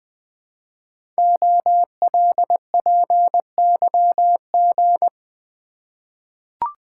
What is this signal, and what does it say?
Morse code, heard as a keyed tone, about 20 wpm, 700 Hz. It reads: OLPYG
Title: Code Groups - 5 Letters - 20wpm